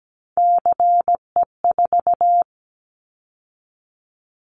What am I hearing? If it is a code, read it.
Morse code, heard as a keyed tone, about 17 wpm, 700 Hz. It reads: CE4
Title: Callsign Prefixes for Top 100 DXCC - 17wpm